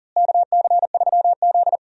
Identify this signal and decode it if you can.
Morse code, heard as a keyed tone, about 40 wpm, 700 Hz. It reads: KC3Z
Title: USA Call Signs - 4 Characters - Repeated 3x - 40wpm